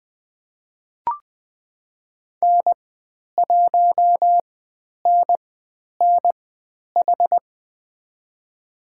Morse code, heard as a keyed tone, about 20 wpm, spaced as for 10 wpm, 700 Hz. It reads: N1NNH
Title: 109 Lesson 18g - Letter m - Call Signs 20wpm